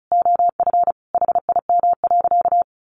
Morse code, heard as a keyed tone, about 34 wpm, 700 Hz. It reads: OF HIM.
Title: Sentences from Top 700 Words 35wpm